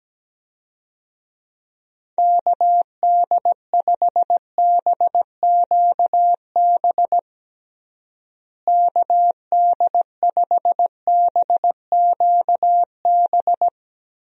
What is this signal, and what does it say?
Morse code, heard as a keyed tone, about 17 wpm, 700 Hz. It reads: KD5BQB KD5BQB
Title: USA Call Signs - 6 Characters - Repeated 2x - 17wpm